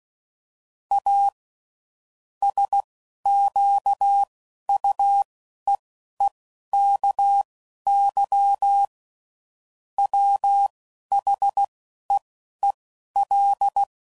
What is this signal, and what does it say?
Morse code, heard as a keyed tone, about 16 wpm, spaced as for 11 wpm, 800 Hz. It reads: A SQUEEKY WHEEL
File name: Sayings / Chuckles / Squeaky